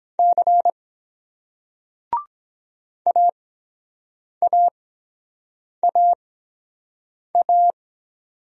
Morse code, heard as a keyed tone, about 23 wpm, spaced as for 7 wpm, 700 Hz. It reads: C AAAA